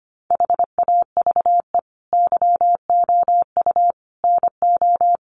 Morse code, heard as a keyed tone, about 25 wpm, 700 Hz. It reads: HAVE YOU NO
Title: Sentences from Top 1500 Words 25wpm